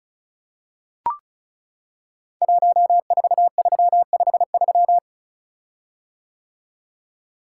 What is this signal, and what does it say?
Morse code, heard as a keyed tone, about 36 wpm, 700 Hz. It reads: 14353
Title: Code Groups - 5 Letter:Numbers - 35wpm